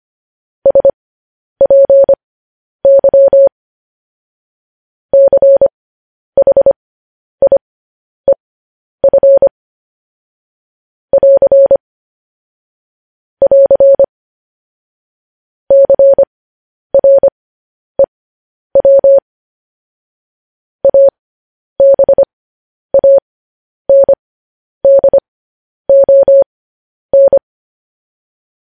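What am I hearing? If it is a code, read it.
Morse code, heard as a keyed tone, about 25 wpm, spaced as for 10 wpm, 550 Hz. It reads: SPY CHIEF <AR> <AR> CREW ABANDON
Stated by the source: Morse code news headlines practice transmission.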